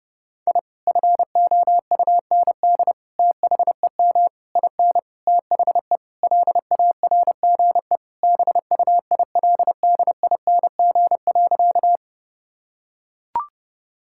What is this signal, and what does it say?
Morse code, heard as a keyed tone, about 30 wpm, 700 Hz. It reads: I FOUND THEM IN THE LARGE BUILDING.
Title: Sentences from Top 1000 Words - Review Part 5 - 30wpm